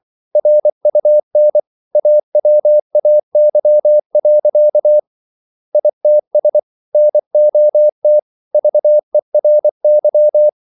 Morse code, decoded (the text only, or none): RUN AWAY. ITS NOT VERY